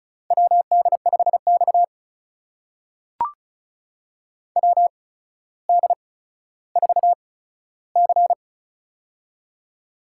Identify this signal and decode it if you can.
Morse code, heard as a keyed tone, about 36 wpm, spaced as for 15 wpm, 700 Hz. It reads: WD5X WD4C
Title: USA Call Signs - 4 Characters - 8x Character Spacing - 35wpm